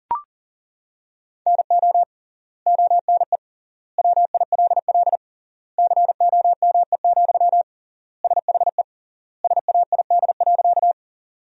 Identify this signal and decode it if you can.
Morse code, heard as a keyed tone, about 40 wpm, 700 Hz. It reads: NO ONE WILL COME, SHE SAID.
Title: 3x Word-Spacing - Sentences from Top 100 Words - 40wpm